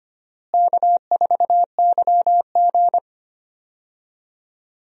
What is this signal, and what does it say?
Morse code, heard as a keyed tone, about 25 wpm, 700 Hz. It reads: K4YG
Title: USA Call Signs - 4 Characters - Repeated 2x - 25wpm